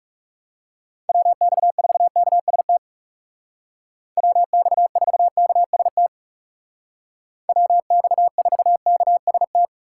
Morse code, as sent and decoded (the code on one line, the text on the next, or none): .-- -..- ....- -.- ... - / .-- -..- ....- -.- ... - / .-- -..- ....- -.- ... -
WX4KST WX4KST WX4KST